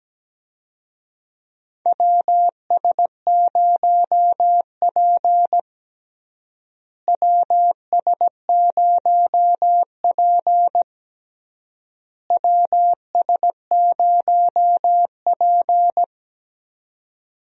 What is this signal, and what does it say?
Morse code, heard as a keyed tone, about 17 wpm, 700 Hz. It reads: WS0P WS0P WS0P
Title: USA Call Signs - 4 Characters - Repeated 3x - 17wpm